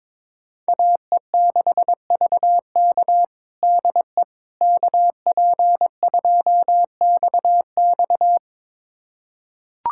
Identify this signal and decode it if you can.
Morse code, heard as a keyed tone, about 22 wpm, 700 Hz. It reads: AE6VK DE KP2XX
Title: USA Callsign DE Callsign - 5 Characters - 22wpm